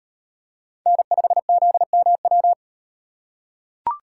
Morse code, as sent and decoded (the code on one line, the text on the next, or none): -. ..... --.. -- .--
N5ZMW